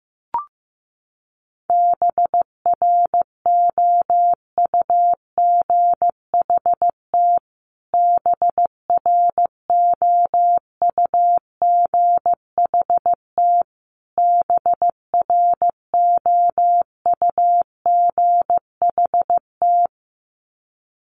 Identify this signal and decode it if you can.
Morse code, heard as a keyed tone, about 15 wpm, 700 Hz. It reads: BROUGHT BROUGHT BROUGHT
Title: Top 500 Words Repeated 15wpm